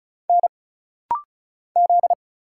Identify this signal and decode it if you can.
Morse code, heard as a keyed tone, about 36 wpm, 700 Hz. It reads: N Z
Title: Single Letter-Number Alt - Rapid-Fire 35wpm